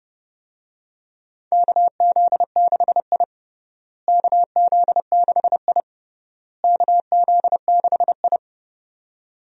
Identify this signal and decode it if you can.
Morse code, heard as a keyed tone, about 30 wpm, 700 Hz. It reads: KZ6I KZ6I KZ6I
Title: USA Call Signs - 4 Characters - Repeated 3x - 30wpm